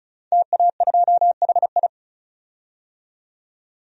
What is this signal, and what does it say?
Morse code, heard as a keyed tone, about 34 wpm, 700 Hz. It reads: TA2HI
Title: DXCC-100 Call Signs - 5 Characters - 35wpm